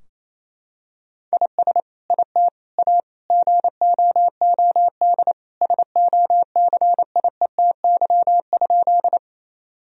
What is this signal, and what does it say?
Morse code, heard as a keyed tone, about 28 wpm, 700 Hz. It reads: IS IT A GOOD SOCIETY?
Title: Sentences from Top 1000 Words - Review Part 4 - 28wpm